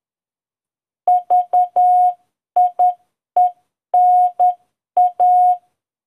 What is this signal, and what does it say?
Morse code, heard as a keyed tone, about 10 wpm, 720 Hz. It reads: VIENA